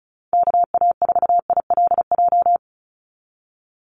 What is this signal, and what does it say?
Morse code, heard as a keyed tone, about 34 wpm, 700 Hz. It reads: KA4ILJ
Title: USA Call Signs 35wpm